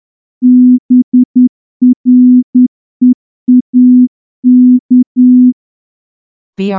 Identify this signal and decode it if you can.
Morse code, heard as a keyed tone, about 9 wpm, 250 Hz. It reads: BREAK
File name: morse-training-10WPM